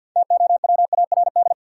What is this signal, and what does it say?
Morse code, heard as a keyed tone, about 53 wpm, 700 Hz. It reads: TOWARD
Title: Sentences from Top 1000 Words - Review Part 6 - New Word Drill - 50wpm